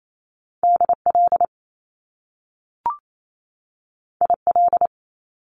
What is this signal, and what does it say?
Morse code, heard as a keyed tone, about 28 wpm, 700 Hz. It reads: DL IL
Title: Top Two Letter Combinations in Words 28wpm